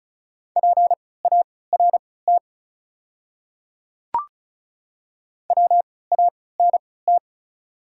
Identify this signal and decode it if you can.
Morse code, heard as a keyed tone, about 36 wpm, spaced as for 20 wpm, 690 Hz. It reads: PART WANT
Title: 3x Character Spacing - Top 300 Words - 35wpm